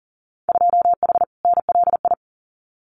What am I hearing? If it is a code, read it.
Morse code, heard as a keyed tone, about 42 wpm, 700 Hz. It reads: 2H NLI
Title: Winter Field Day Exchanges Encore 40wpm